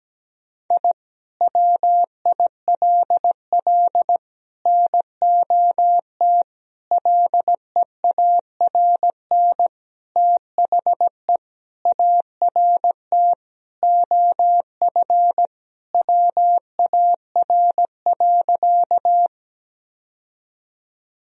Morse code, heard as a keyed tone, about 17 wpm, 700 Hz. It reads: I WILL NOT LEARN THE ART OF WAR.
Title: Sentences from Top 1000 Words - Review Part 3 - 17wpm